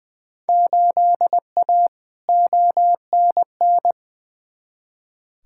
Morse code, heard as a keyed tone, about 20 wpm, 700 Hz. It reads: 8A ONN
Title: Field Day Exchanges Encore - 20wpm